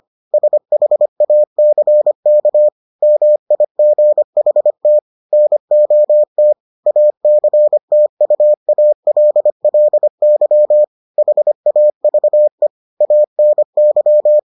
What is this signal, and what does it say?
Morse code, heard as a keyed tone, about 25 wpm, 600 Hz. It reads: SHACK MIGHT NOT ACTUALLY HAVE ANY